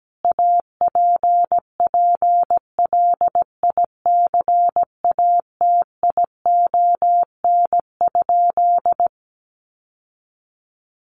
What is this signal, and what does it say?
Morse code, heard as a keyed tone, about 17 wpm, 700 Hz. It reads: APPLICATION?
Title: Sentences from Top 1200 Words 17wpm